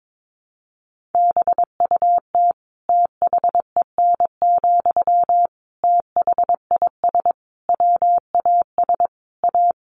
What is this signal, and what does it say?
Morse code, heard as a keyed tone, about 22 wpm, 700 Hz. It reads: BUT THEN, THIS WAS A
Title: Sentences from Top 1400 Words 22wpm